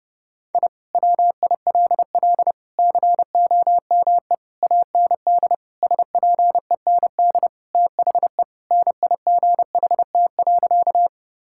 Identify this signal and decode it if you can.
Morse code, heard as a keyed tone, about 30 wpm, 700 Hz. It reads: I WILL COME AND SPEND THE NIGHT.
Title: Sentences from Top 1000 Words - Review Part 5 - 30wpm